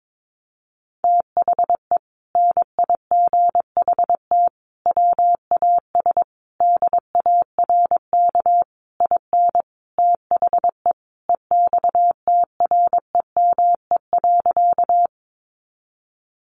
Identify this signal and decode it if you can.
Morse code, heard as a keyed tone, about 22 wpm, 700 Hz. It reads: THE NIGHT WAS DARK IN THE EXTREME.
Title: Sentences from Top 1300 Words 22wpm